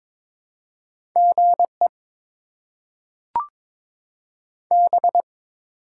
Morse code, heard as a keyed tone, about 22 wpm, 700 Hz. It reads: G E B
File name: Top 90 QSO Elements - 8x Character Spacing - 22wpm